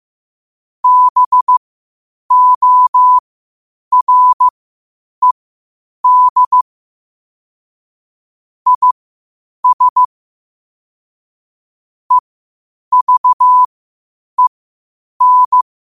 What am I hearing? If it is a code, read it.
Morse code, heard as a keyed tone, about 15 wpm, spaced as for 8 wpm, 1000 Hz. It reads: BORED IS EVEN
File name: qotdfiles / 8wpm